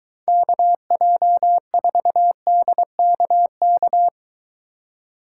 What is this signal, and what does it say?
Morse code, heard as a keyed tone, about 23 wpm, 700 Hz. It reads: KJ4DKK